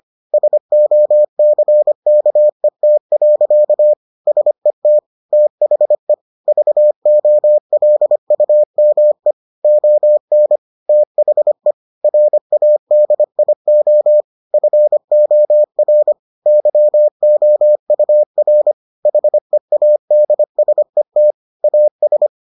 Morse code, heard as a keyed tone, about 25 wpm, 600 Hz. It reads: SOCKET. SET THE VOLUME ON THE RADIO FOR YOUR HEADSET AS